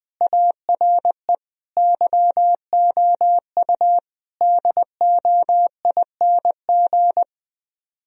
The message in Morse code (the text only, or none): ARE YOU DOING